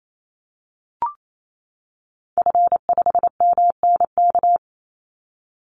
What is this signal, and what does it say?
Morse code, heard as a keyed tone, about 28 wpm, 700 Hz. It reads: F5MNK
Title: Call Signs - 28wpm